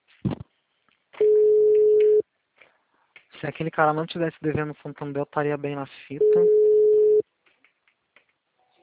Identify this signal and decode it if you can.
Ringback tone (Europe, 425 Hz), with speech.
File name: out- -2024- - - .84308